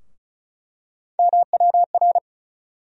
Morse code, heard as a keyed tone, about 34 wpm, 700 Hz. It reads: MWR
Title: USA Call Sign Postfixes - 3 Letters 35wpm